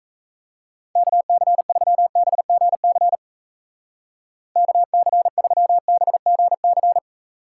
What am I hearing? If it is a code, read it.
Morse code, heard as a keyed tone, about 36 wpm, 700 Hz. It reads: KC3BGC KC3BGC